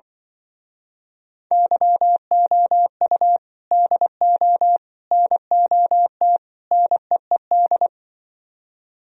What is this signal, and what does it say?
Morse code, heard as a keyed tone, about 24 wpm, 700 Hz. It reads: YOU DO NOT NEED
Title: Speed-Racing - Sets of 4 Words 20wpm